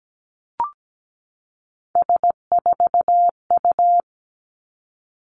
Morse code, heard as a keyed tone, about 17 wpm, 700 Hz. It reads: S4U
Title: Three Challenging Letter-Number Combinations 17wpm